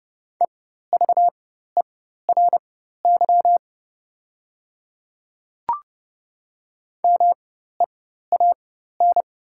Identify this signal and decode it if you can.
Morse code, heard as a keyed tone, about 30 wpm, spaced as for 14 wpm, 700 Hz. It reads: EVERY MEAN
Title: 4x Character Spacing - Top 200 Words - 30wpm